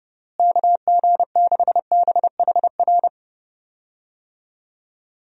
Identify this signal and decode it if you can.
Morse code, heard as a keyed tone, about 30 wpm, 700 Hz. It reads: KG6BHR
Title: USA Call Signs - 6 Characters - Repeated 3x - 30wpm